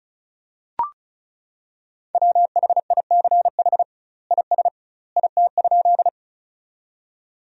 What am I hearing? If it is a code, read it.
Morse code, heard as a keyed tone, about 36 wpm, 700 Hz. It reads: WHICH IS IT?
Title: 2x Word-Spacing - Sentences from Top 100 Words - 35wpm